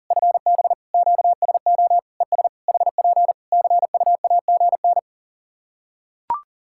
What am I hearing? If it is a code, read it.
Morse code, heard as a keyed tone, about 40 wpm, 700 Hz. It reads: FB QSO ES HP CUAGN
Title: K1USN SST QSO Practice 40wpm